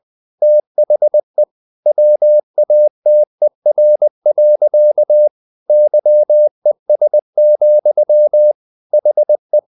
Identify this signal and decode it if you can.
Morse code, heard as a keyed tone, about 20 wpm, 600 Hz. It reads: THE WATER. YES, HE
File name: the old man and the sea part 2 20wpm